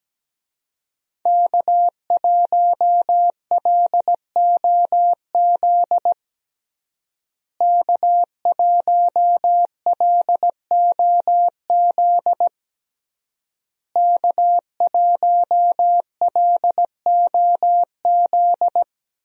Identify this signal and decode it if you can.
Morse code, heard as a keyed tone, about 17 wpm, 700 Hz. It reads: K1LOZ K1LOZ K1LOZ
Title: USA Call Signs - 5 Characters - Repeated 3x - 17wpm